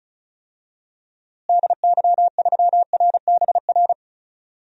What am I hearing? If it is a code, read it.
Morse code, heard as a keyed tone, about 36 wpm, 700 Hz. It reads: DY3RBR
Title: DXCC-100 Call Signs - 6 Characters - Repeated 2x - 35wpm